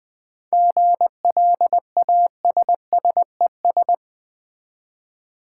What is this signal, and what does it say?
Morse code, heard as a keyed tone, about 20 wpm, 700 Hz. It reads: GLASSES
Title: Easy Words with Hints - 20wpm